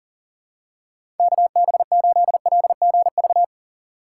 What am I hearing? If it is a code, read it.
Morse code, heard as a keyed tone, about 37 wpm, 700 Hz. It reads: KB8LGV
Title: USA Call Signs - 6 Characters - 4x Character Spacing - 40wpm